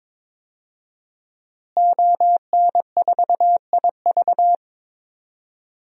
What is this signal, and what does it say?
Morse code, heard as a keyed tone, about 22 wpm, 700 Hz. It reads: ON4IV
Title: DXCC-100 Call Signs - 5 Characters - Repeated 3x - 22wpm